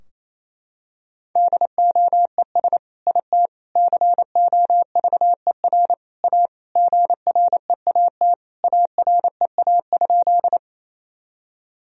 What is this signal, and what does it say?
Morse code, heard as a keyed tone, about 28 wpm, 700 Hz. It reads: DOES IT COVER A GREAT AREA?
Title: Sentences from Top 1000 Words - Review Part 2 - 28wpm